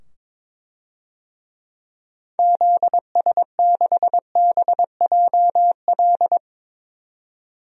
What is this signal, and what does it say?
Morse code, heard as a keyed tone, about 22 wpm, 700 Hz. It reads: ZS6BJL
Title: DXCC-100 Call Signs - 6 Characters - 22wpm